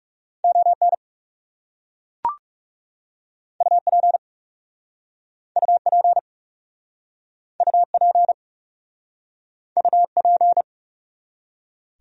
Morse code, heard as a keyed tone, about 37 wpm, 700 Hz. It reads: ON UP UP UP UP
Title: Speed-Racing Top 100 Words 30wpm